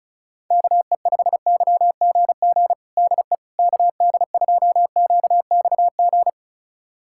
Morse code, heard as a keyed tone, about 34 wpm, 700 Hz. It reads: KE5YGG DE KD2QXG
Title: USA Callsign DE Callsign - 6 Characters - 35wpm